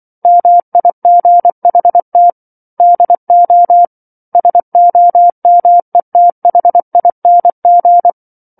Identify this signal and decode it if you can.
Morse code, heard as a keyed tone, about 24 wpm, 700 Hz. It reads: MIGHT DO SOMETHING